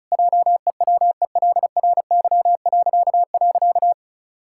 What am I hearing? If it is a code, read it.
Morse code, heard as a keyed tone, about 34 wpm, 700 Hz. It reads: JEWELRY..
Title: Pangram Challenge v2 - 35wpm